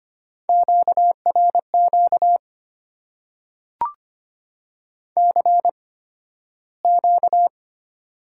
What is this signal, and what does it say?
Morse code, heard as a keyed tone, about 25 wpm, spaced as for 13 wpm, 700 Hz. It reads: QRQ CQ
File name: Top 90 QSO Elements - 8x Character Spacing - 25wpm